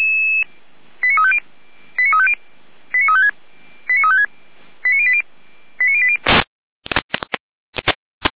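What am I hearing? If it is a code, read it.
Five-tone selective call (ZVEI), heard as quick runs of five tones, tones of about 70 ms. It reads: 89350 89350 89356 89356 89080 89080
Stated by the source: ZVEI five-tone call 89409 received on SDR.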